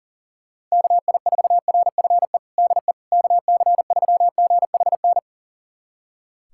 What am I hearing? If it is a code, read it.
Morse code, heard as a keyed tone, about 40 wpm, 700 Hz. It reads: KI4RFE DE KC3GHN
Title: USA Callsign DE Callsign - 6 Characters - 40wpm